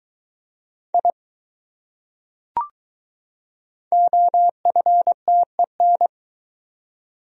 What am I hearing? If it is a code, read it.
Morse code, heard as a keyed tone, about 21 wpm, 700 Hz. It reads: I OFTEN